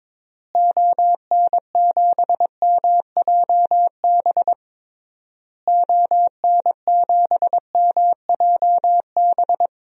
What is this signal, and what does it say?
Morse code, heard as a keyed tone, about 22 wpm, 700 Hz. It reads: ON7MJB ON7MJB
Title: DXCC-100 Call Signs - 6 Characters - Repeated 3x - 22wpm